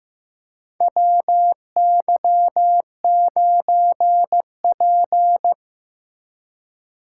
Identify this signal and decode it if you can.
Morse code, heard as a keyed tone, about 15 wpm, 700 Hz. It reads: WY9P
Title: USA Call Signs - 4 Characters - 3x Character Spacing - 15wpm